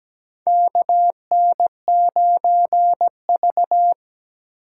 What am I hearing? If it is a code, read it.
Morse code, heard as a keyed tone, about 17 wpm, 700 Hz. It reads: KN9V
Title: USA Call Signs - 3x Character Spacing - 17wpm